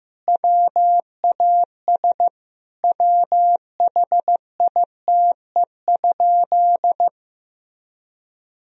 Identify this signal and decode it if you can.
Morse code, heard as a keyed tone, about 15 wpm, 700 Hz. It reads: WAS WHITE?
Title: Sentences from Top 1000 Words - Review Part 2 - 15wpm